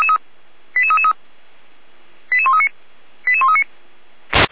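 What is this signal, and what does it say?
Five-tone selective call (ZVEI), heard as quick runs of five tones, tones of about 70 ms. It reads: 88333 88139 88139
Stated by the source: ZVEI five-tone selective call received on SDR, from the Feuerwehr Orlamünde fire brigade server.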